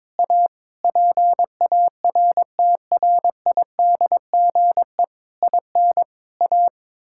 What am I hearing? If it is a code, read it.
Morse code, heard as a keyed tone, about 21 wpm, 700 Hz. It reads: A PARTRIDGE IN A
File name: Sentences with 9-Word Phrases - 22wpm